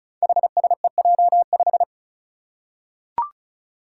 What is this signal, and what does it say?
Morse code, heard as a keyed tone, about 34 wpm, 700 Hz. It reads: HSEJ5
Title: Five Challenging Letter-Number Combinations 35wpm